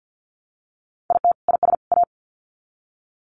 Morse code, heard as a keyed tone, about 50 wpm, 700 Hz. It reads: IT IS A
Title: Sets of 3 Words 50wpm